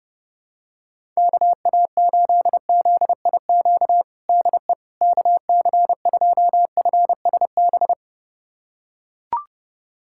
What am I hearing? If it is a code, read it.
Morse code, heard as a keyed tone, about 32 wpm, 700 Hz. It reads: KA8ZIQ DE KC2FSB
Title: USA Callsign DE Callsign 30wpm